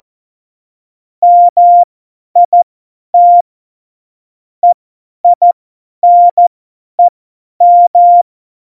Morse code, heard as a keyed tone, about 14 wpm, spaced as for 10 wpm, 700 Hz. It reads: MIT EINEM
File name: Morsekurs / Witze / Witz 13. August 2019, 50 F70